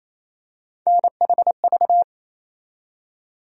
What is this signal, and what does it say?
Morse code, heard as a keyed tone, about 28 wpm, 700 Hz. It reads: NHV
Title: USA Call Sign Postfixes - 3 Letters 28wpm